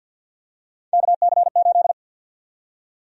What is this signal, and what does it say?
Morse code, heard as a keyed tone, about 51 wpm, 700 Hz. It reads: KC8
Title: USA Call Sign Prefixes - 50wpm